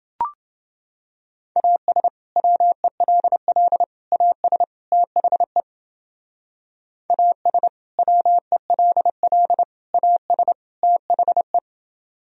Morse code, heard as a keyed tone, about 28 wpm, 700 Hz. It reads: AS WELL AS THE AS WELL AS THE